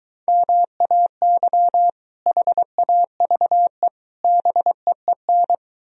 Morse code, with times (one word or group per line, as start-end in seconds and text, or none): 0.3-1.9 MAY
2.3-3.9 HAVE
4.2-5.5 BEEN